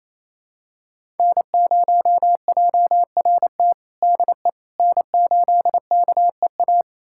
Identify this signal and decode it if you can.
Morse code, heard as a keyed tone, about 28 wpm, 700 Hz. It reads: N0JRT DE N8KEA
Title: USA Callsign DE Callsign - 5 Characters - 28wpm